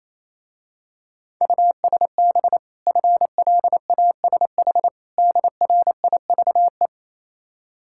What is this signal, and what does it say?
Morse code, heard as a keyed tone, about 27 wpm, 690 Hz. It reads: USB FLASH DRIVE
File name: Easy Words - Speed-Racing with Hints - 22wpm